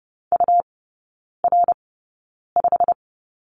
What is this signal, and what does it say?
Morse code, heard as a keyed tone, about 30 wpm, spaced as for 9 wpm, 700 Hz. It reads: UR5
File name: Lesson 19e - Letter w - Three Characters 30wpm